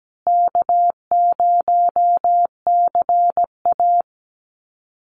Morse code, heard as a keyed tone, about 17 wpm, 700 Hz. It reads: K0CA
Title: Call Signs Repeated 2x - 17wpm